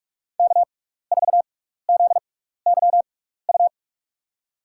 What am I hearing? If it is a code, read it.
Morse code, heard as a keyed tone, about 42 wpm, spaced as for 15 wpm, 700 Hz. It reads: K4ZYU
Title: USA Call Signs - 5 Characters - 6x Character Spacing - 45wpm